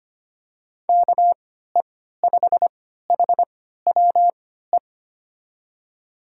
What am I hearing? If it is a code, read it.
Morse code, heard as a keyed tone, about 25 wpm, spaced as for 14 wpm, 700 Hz. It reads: KE5HWE
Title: USA Call Signs - 6 Characters - 3x Character Spacing - 25wpm